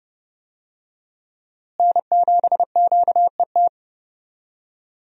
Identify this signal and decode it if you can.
Morse code, heard as a keyed tone, about 30 wpm, 700 Hz. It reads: N7QET
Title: USA Call Signs - 5 Characters - Repeated 2x - 30wpm